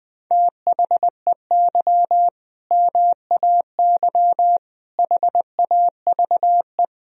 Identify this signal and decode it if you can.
Morse code, heard as a keyed tone, about 20 wpm, 700 Hz. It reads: THEY MAY HAVE
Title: Sentences from Top 1000 Words - Review Part 2 - 20wpm